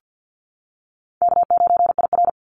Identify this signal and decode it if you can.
Morse code, heard as a keyed tone, about 50 wpm, 700 Hz. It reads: K9IR
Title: NA SOTA Activator Callsigns 50wpm